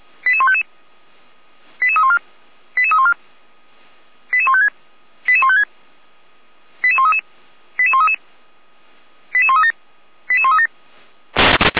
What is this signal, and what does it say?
Five-tone selective call (ZVEI), heard as quick runs of five tones, tones of about 70 ms. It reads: 88155 88314 88314 88156 88156 88133 88133 88137 88137
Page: feuerwehr-orlamuende